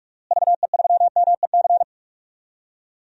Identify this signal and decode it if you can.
Morse code, heard as a keyed tone, about 48 wpm, 700 Hz. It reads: VE3MEC
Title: DXCC-100 Call Signs - 6 Characters - Repeated 3x - 45wpm